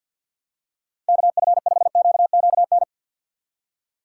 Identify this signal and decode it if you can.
Morse code, heard as a keyed tone, about 53 wpm, 700 Hz. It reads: KF5QQN
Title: USA Call Signs - 6 Characters - 4x Character Spacing - 50wpm